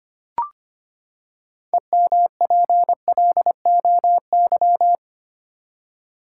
Morse code, heard as a keyed tone, about 25 wpm, 700 Hz. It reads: EMPLOY
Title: Six Letter Words - 25wpm